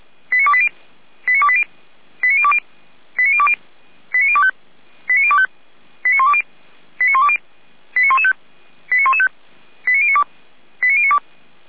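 Five-tone selective call (ZVEI), heard as quick runs of five tones, tones of about 70 ms. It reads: 89280 89280 89020 89020 89025 89025 89120 89120 89115 89115 89092 89092
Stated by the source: ZVEI five-tone call received by SDR, from feuerwehr-orlamuende.org fire brigade audio archive.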